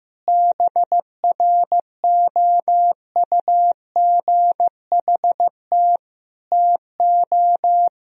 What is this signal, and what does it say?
Morse code, heard as a keyed tone, about 15 wpm, 700 Hz. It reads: BROUGHT TO
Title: Sentences from Top 1000 Words - Review Part 2 - 15wpm